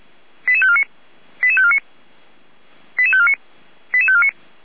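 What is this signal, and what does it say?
Five-tone selective call (ZVEI), heard as quick runs of five tones, tones of about 70 ms. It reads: 88549 88549 88549 88549
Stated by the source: ZVEI five-tone selective call received via SDR.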